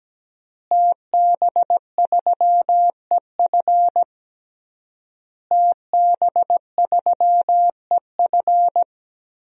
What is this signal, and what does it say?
Morse code, heard as a keyed tone, about 17 wpm, 700 Hz. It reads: TB3EF TB3EF
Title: DXCC-100 Call Signs - 5 Characters - Repeated 3x - 17wpm